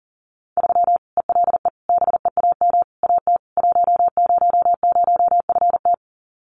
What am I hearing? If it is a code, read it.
Morse code, heard as a keyed tone, about 40 wpm, 700 Hz. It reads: 3 ELE BEAM AT 100FT
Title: Common QSO Phrases 40wpm